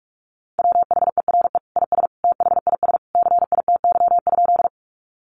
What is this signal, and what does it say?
Morse code, heard as a keyed tone, about 44 wpm, 700 Hz. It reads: WHERE IS THIS CITY?
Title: Sentences from Top 300 Words 45wpm